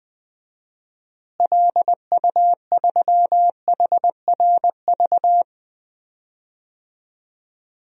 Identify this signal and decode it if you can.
Morse code, heard as a keyed tone, about 20 wpm, 700 Hz. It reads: LU3HRV
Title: DXCC-100 Call Signs - 6 Characters - Repeated 3x - 20wpm